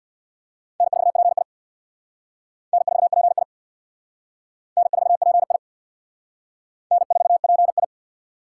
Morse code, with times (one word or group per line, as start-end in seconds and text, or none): 0.8-1.4 N4PI
2.7-3.4 N4PI
4.8-5.6 N4PI
6.9-7.8 N4PI